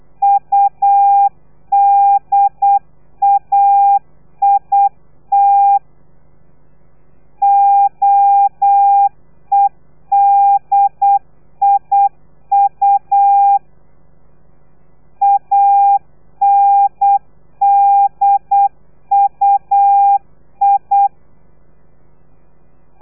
Morse code, heard as a keyed tone, about 8 wpm, 800 Hz. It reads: UDAIT OEDIU ANDUI
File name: telegrafia / i2tfj / Lez3 35